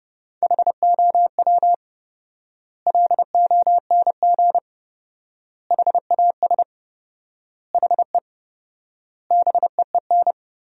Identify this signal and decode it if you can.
Morse code, heard as a keyed tone, about 30 wpm, 700 Hz. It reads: HOW LONG HAS HE BEEN
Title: 4x Word-Spacing - Sentences from Top 100 Words - 30wpm